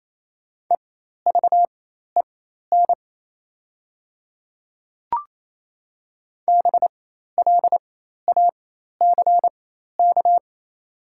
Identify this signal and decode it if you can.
Morse code, heard as a keyed tone, about 28 wpm, spaced as for 13 wpm, 700 Hz. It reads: EVEN BLACK
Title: 4x Character Spacing - Top 500 Words - 28wpm